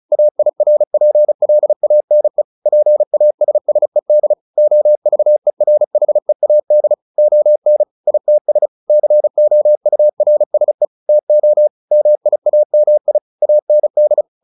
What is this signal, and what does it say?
Morse code, heard as a keyed tone, about 34 wpm, 600 Hz. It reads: AIRPLANE PASSED OVERHEAD ON ITS COURSE TO MIAMI AND
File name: the old man and the sea 2 35wpm